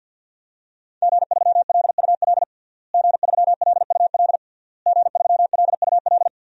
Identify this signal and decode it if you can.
Morse code, heard as a keyed tone, about 53 wpm, 700 Hz. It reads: G3LUL G3LUL G3LUL
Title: DXCC-100 Call Signs - 5 Characters - Repeated 3x - 50wpm